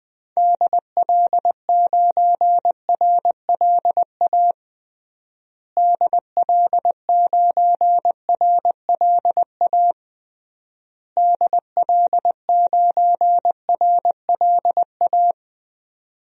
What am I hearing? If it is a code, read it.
Morse code, heard as a keyed tone, about 20 wpm, 700 Hz. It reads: DL9RLA DL9RLA DL9RLA
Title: DXCC-100 Call Signs - 6 Characters - Repeated 3x - 20wpm